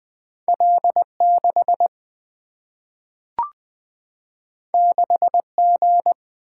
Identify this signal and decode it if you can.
Morse code, heard as a keyed tone, about 20 wpm, 700 Hz. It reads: L6 6G
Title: Single-Letter Single-Number 20wpm